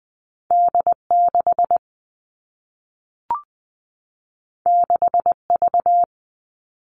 Morse code, heard as a keyed tone, about 20 wpm, 700 Hz. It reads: D 6 6 V
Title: All Cut Numbers - 2-Digit 20wpm